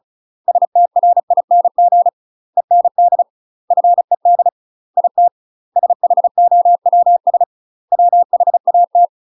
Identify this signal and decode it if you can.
Morse code, heard as a keyed tone, about 36 wpm, 700 Hz. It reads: STRING END FED IT SHOWS WHAT
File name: UK MASTER QSOs TRAINING / 35WPM / UK MASTER QSOs 0023